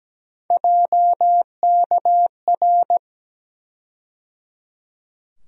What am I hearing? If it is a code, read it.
Morse code, heard as a keyed tone, about 17 wpm, 700 Hz. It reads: JKR
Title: USA Call Sign Postfixes - 3 Letters 17wpm